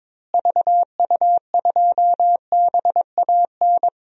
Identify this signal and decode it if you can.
Morse code, heard as a keyed tone, about 22 wpm, 700 Hz. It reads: VU2BAN
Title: DXCC-100 Call Signs - 6 Characters - Repeated 3x - 22wpm